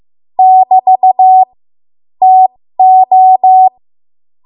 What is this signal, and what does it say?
Morse code, heard as a keyed tone, about 15 wpm, spaced as for 13 wpm, 750 Hz. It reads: <BT> TO
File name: Archive / 13 WPM / 13WPM